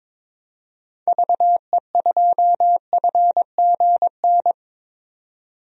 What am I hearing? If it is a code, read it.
Morse code, heard as a keyed tone, about 22 wpm, 700 Hz. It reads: VE2FGN
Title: DXCC-100 Call Signs - 6 Characters - 22wpm